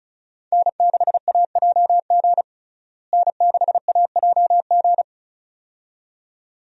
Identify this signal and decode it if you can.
Morse code, heard as a keyed tone, about 36 wpm, 700 Hz. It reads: N6AJG N6AJG
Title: USA Call Signs - 5 Characters - Repeated 2x - 35wpm